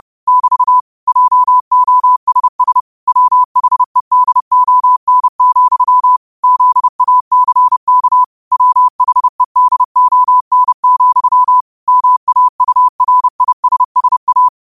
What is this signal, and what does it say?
Morse code, heard as a keyed tone, about 30 wpm, 1000 Hz. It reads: <BT> JOSS WHEDON, ZACK WHEDON, MAURISSA
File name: qotdfiles / 30wpm